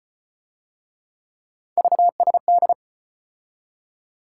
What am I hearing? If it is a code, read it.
Morse code, heard as a keyed tone, about 32 wpm, 700 Hz. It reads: VSD